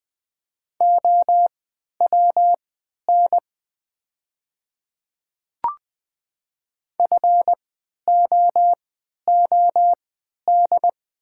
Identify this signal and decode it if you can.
Morse code, heard as a keyed tone, about 20 wpm, spaced as for 11 wpm, 700 Hz. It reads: OWN FOOD
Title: 3x Character Spacing - Top 500 Words - 20wpm